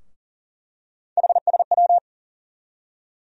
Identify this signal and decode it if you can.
Morse code, heard as a keyed tone, about 37 wpm, 700 Hz. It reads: HSW
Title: USA Call Sign Postfixes - 3 Letters 40wpm